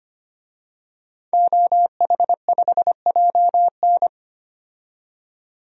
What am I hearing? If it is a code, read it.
Morse code, heard as a keyed tone, about 25 wpm, 700 Hz. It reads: OH5JN
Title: DXCC-100 Call Signs - 5 Characters - 25wpm